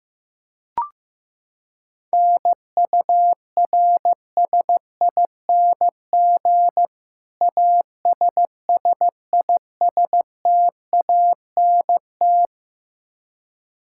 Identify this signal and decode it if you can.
Morse code, heard as a keyed tone, about 15 wpm, 700 Hz. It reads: NURSING ASSISTANT
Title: Practice QSO - Exchange 3 - More Info - 15wpm